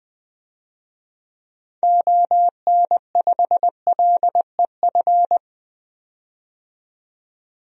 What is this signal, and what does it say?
Morse code, heard as a keyed tone, about 20 wpm, 700 Hz. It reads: ON5LEF
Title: DXCC-100 Call Signs - 6 Characters - 20wpm